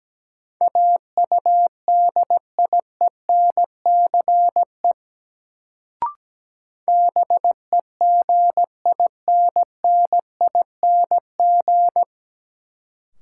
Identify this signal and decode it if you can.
Morse code, heard as a keyed tone, about 17 wpm, 700 Hz. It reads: AUDIENCE BEGINNING
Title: Sentences from Top 800 Words - New Word Drill - 17wpm